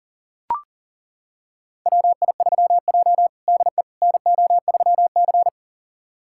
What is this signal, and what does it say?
Morse code, heard as a keyed tone, about 37 wpm, 690 Hz. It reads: WI3J DE NO3C
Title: USA Callsign DE Callsign - 4 Characters - 40wpm